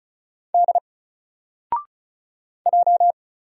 Morse code, heard as a keyed tone, about 36 wpm, 700 Hz. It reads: D J
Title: Single Character (Letter:Number:Punctuation:Prosign) Alt 35wpm